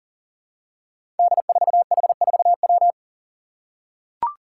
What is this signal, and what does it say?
Morse code, heard as a keyed tone, about 40 wpm, 690 Hz. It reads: D4H4W
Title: Five Challenging Letter-Number Combinations 40wpm